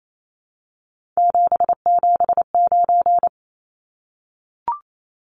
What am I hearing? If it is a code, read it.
Morse code, heard as a keyed tone, about 28 wpm, 700 Hz. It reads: 779
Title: Numbers - 3-Digit 28wpm